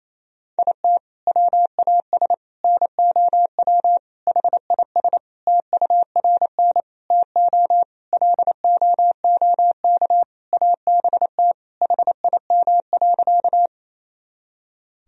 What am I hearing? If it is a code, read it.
Morse code, heard as a keyed tone, about 28 wpm, 700 Hz. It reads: IT WAS NOW HIS TURN TO LOOK ABT HIM.
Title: Sentences from Top 200 Words and Abbreviations 28wpm